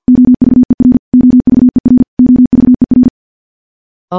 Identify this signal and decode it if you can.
Morse code, heard as a keyed tone, about 47 wpm, 250 Hz. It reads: OVER OVER OVER
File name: morse / morse-training-50WPM